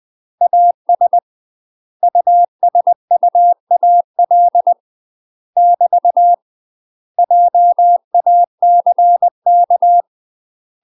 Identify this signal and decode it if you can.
Morse code, heard as a keyed tone, about 20 wpm, 700 Hz. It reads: AS USUAL <BT> JACK
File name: UK MASTER QSOs TRAINING / 20WPM / UK MASTER QSOs 0012